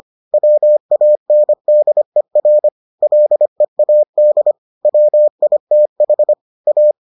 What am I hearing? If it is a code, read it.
Morse code, heard as a keyed tone, about 25 wpm, 600 Hz. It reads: WANDER LEAD WITH A